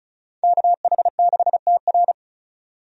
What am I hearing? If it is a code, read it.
Morse code, heard as a keyed tone, about 34 wpm, 700 Hz. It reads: KH6TR
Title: DXCC-100 Call Signs - 5 Characters - 35wpm